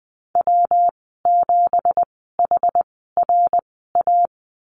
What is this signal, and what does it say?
Morse code, heard as a keyed tone, about 20 wpm, spaced as for 14 wpm, 700 Hz. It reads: W7HRA
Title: 186 Lesson 30g - Letter b - Call Signs 20wpm